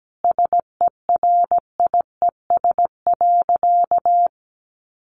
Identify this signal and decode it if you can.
Morse code, heard as a keyed tone, about 17 wpm, 700 Hz. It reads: SERIES.
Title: Sentences from Top 1200 Words 17wpm